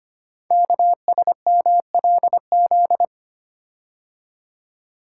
Morse code, heard as a keyed tone, about 25 wpm, 700 Hz. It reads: KSMLZ
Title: Code Groups - 5 Letters - 25wpm